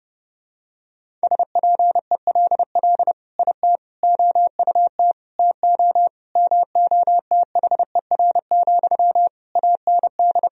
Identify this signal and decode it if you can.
Morse code, heard as a keyed tone, about 32 wpm, 700 Hz. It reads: SPELL IT OUT TO MOTHER, AND
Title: Sentences from Top 1000 Words - Review Part 1 - 30wpm